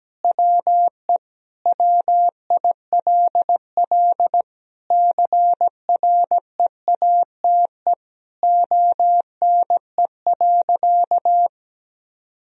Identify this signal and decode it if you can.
Morse code, heard as a keyed tone, about 17 wpm, 700 Hz. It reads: WE WILL CREATE ONE.
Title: Sentences from Top 1000 Words - Review Part 3 - 17wpm